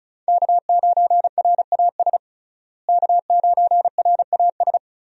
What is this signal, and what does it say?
Morse code, heard as a keyed tone, about 36 wpm, 700 Hz. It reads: K9RAS K9RAS
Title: USA Call Signs - 5 Characters - Repeated 2x - 35wpm